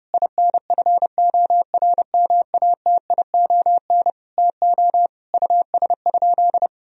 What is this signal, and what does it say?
Morse code, heard as a keyed tone, about 30 wpm, 700 Hz. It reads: INFORMATION TO US?
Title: Sentences from Top 1000 Words - Review Part 3 - 30wpm